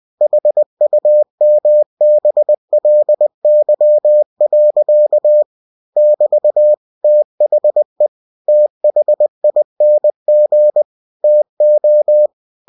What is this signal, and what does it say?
Morse code, heard as a keyed tone, about 20 wpm, 600 Hz. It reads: HUMBLY. <BT> THE THING TO
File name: cw-books / winnie-the-pooh / winnie-20wpm-page-0030